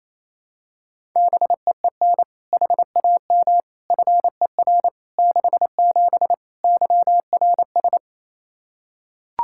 Morse code, heard as a keyed tone, about 28 wpm, 700 Hz. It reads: BEEN HAM FER 67 YRS
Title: Practice QSO - Exchange 3 - More Info - 28wpm